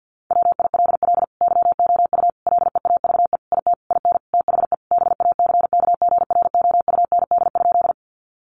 Morse code, heard as a keyed tone, about 50 wpm, 700 Hz. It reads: WILL YOU LEAVE IT IN THE BACKGROUND?
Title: Sentences from Top 1300 Words 50wpm